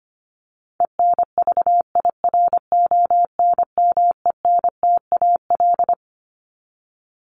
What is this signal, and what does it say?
Morse code, heard as a keyed tone, about 25 wpm, 700 Hz. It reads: ENVIRONMENTAL
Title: Top 1001-1250 Words 25wpm